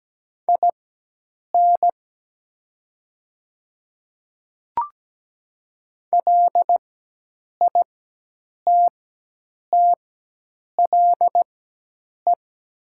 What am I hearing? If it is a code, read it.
Morse code, heard as a keyed tone, about 17 wpm, spaced as for 8 wpm, 700 Hz. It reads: IN LITTLE
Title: 4x Character Spacing - Top 200 Words - 17wpm